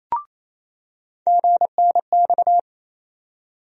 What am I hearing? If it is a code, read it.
Morse code, heard as a keyed tone, about 27 wpm, 700 Hz. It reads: GNX
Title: USA Call Sign Postfixes - 3 Letters 28wpm